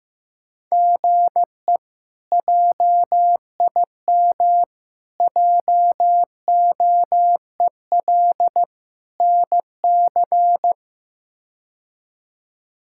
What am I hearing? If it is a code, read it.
Morse code, heard as a keyed tone, about 16 wpm, 700 Hz. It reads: GE JIM JOEL NC
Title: K1USN SST QSO Practice - 15wpm